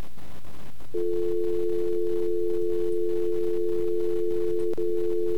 Dial tone (North America), 350 + 440 Hz.